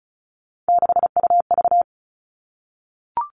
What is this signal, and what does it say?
Morse code, heard as a keyed tone, about 34 wpm, 700 Hz. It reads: 6UV
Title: All Cut Numbers - 3-Digit 35wpm